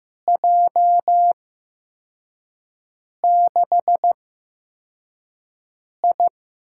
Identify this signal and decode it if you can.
Morse code, heard as a keyed tone, about 15 wpm, spaced as for 4 wpm, 700 Hz. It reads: J6I
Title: USA Call Signs - 8x Character Spacing - 15wpm